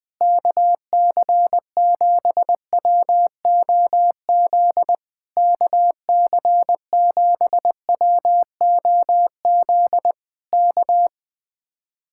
Morse code, heard as a keyed tone, about 20 wpm, 700 Hz. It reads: KC7WOZ KC7WOZ K
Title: Speed-Racing Common QSO Phrases - 17wpm